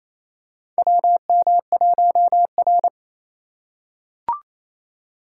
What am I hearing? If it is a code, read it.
Morse code, heard as a keyed tone, about 28 wpm, 700 Hz. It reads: WM1R
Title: USA Call Signs - 4 Characters - 3x Character Spacing - 28wpm